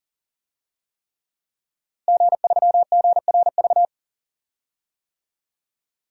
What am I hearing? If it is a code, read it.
Morse code, heard as a keyed tone, about 40 wpm, 700 Hz. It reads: G3GRV
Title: DXCC-100 Call Signs - 5 Characters - Repeated 3x - 40wpm